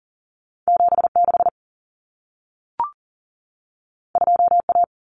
Morse code, heard as a keyed tone, about 40 wpm, 700 Hz. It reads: 76 2A
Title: Common Cut Numbers - 2-Digit 40wpm